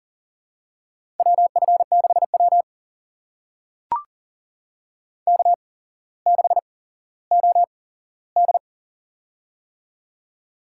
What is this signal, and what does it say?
Morse code, heard as a keyed tone, about 42 wpm, spaced as for 17 wpm, 690 Hz. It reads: WF6W K6OD
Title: USA Call Signs - 4 Characters - 8x Character Spacing - 40wpm